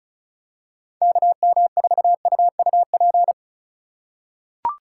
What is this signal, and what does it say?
Morse code, heard as a keyed tone, about 34 wpm, 700 Hz. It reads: KM4UUP
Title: USA Call Signs - 6 Characters - 2x Character Spacing - 35wpm